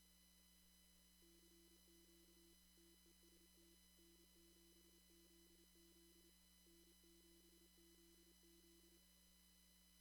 Morse code, heard as a keyed tone, about 27 wpm, 340 Hz. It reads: CQ TEST N7ZZ N7ZZ
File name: CW / kt5j wpx cw-op-k5pi